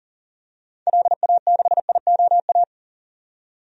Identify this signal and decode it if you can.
Morse code, heard as a keyed tone, about 40 wpm, 700 Hz. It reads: LA6IOA
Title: DXCC-100 Call Signs - 6 Characters - 40wpm